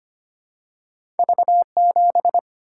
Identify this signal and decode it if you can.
Morse code, heard as a keyed tone, about 25 wpm, 700 Hz. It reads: V 7
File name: Two Letter-Number Combinations - 25wpm